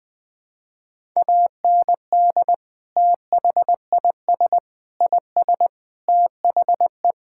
Morse code, heard as a keyed tone, about 21 wpm, 700 Hz. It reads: AND THIS IS THE
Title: Sentences from Top 1000 Words - Review Part 5 - 20wpm